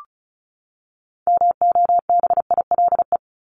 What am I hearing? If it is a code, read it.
Morse code, heard as a keyed tone, about 34 wpm, 700 Hz. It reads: MOBILE
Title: Top 1751-2000 Words 35wpm